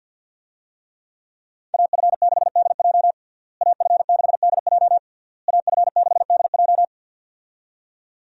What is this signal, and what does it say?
Morse code, heard as a keyed tone, about 52 wpm, 700 Hz. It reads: AF6DJ AF6DJ AF6DJ
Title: USA Call Signs - 5 Characters - Repeated 3x - 50wpm